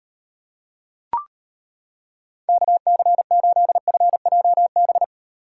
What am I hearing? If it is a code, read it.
Morse code, heard as a keyed tone, about 31 wpm, 700 Hz. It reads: KC8FJB
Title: Speed Racing - USA 2x3 Call Signs 25wpm